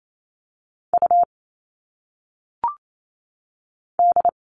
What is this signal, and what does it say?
Morse code, heard as a keyed tone, about 28 wpm, 700 Hz. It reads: U D
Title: All Cut Numbers - 1-Digit 28wpm